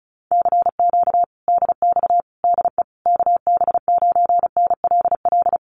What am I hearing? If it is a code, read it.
Morse code, heard as a keyed tone, about 34 wpm, 700 Hz. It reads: CQ DX DE KB9NLL
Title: Common QSO Phrases Encore 35wpm